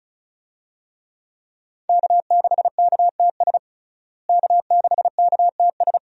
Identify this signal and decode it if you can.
Morse code, heard as a keyed tone, about 35 wpm, 700 Hz. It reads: K6KTS K6KTS
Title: USA Call Signs - 5 Characters - Repeated 2x - 35wpm